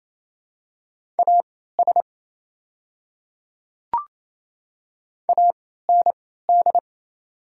Morse code, heard as a keyed tone, about 28 wpm, spaced as for 16 wpm, 700 Hz. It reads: AS AND
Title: 3x Character Spacing - Top 100 Words - 28wpm